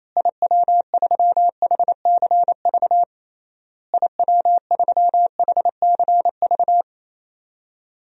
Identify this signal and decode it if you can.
Morse code, heard as a keyed tone, about 28 wpm, 700 Hz. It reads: IW3HCV IW3HCV
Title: DXCC-100 Call Signs - 6 Characters - Repeated 3x - 28wpm